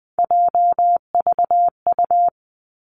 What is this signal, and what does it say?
Morse code, heard as a keyed tone, about 20 wpm, 700 Hz. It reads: JVU
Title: Three Challenging Letter-Number Combinations 20wpm